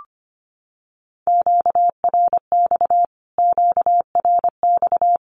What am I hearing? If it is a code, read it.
Morse code, heard as a keyed tone, about 25 wpm, 700 Hz. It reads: QRX QRX
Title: Q-Codes Repeated 3x 25wpm